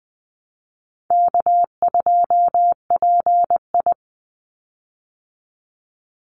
Morse code, heard as a keyed tone, about 20 wpm, 700 Hz. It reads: K2PI
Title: Easy Call Signs Repeated 2x - 20wpm